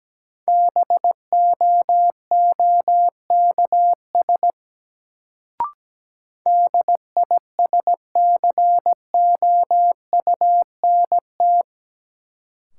Morse code, heard as a keyed tone, about 17 wpm, 700 Hz. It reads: BOOKS DISCOUNT
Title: Sentences from Top 1200 Words - New Word Drill Alt - 17wpm